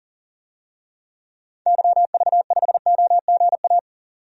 Morse code, heard as a keyed tone, about 38 wpm, 700 Hz. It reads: YV5OGA
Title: DXCC-100 Call Signs - 6 Characters - 40wpm